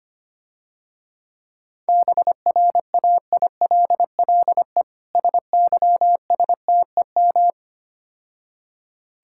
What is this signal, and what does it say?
Morse code, heard as a keyed tone, about 26 wpm, 700 Hz. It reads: BRAILLE SYSTEM
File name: practice / 25wpm / Difficult Words - Speed-Racing with Hints - 25wpm